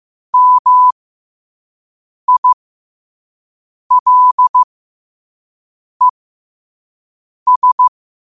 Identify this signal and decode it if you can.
Morse code, heard as a keyed tone, about 15 wpm, spaced as for 5 wpm, 1000 Hz. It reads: MILES
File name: qotdfiles / 5wpm